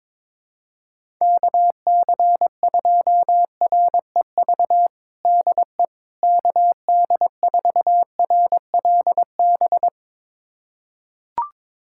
Morse code, heard as a keyed tone, about 22 wpm, 700 Hz. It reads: KC2REV DE KD4RLB
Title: USA Callsign DE Callsign - 6 Characters - 22wpm